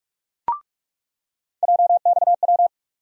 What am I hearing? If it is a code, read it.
Morse code, heard as a keyed tone, about 47 wpm, 700 Hz. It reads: JXW
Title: USA Call Sign Postfixes - 3 Letters 45wpm